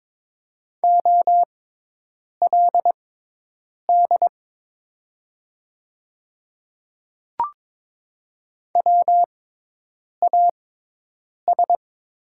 Morse code, heard as a keyed tone, about 22 wpm, spaced as for 8 wpm, 690 Hz. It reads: OLD WAS
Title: 6x Character Spacing - Top 300 Words - 22wpm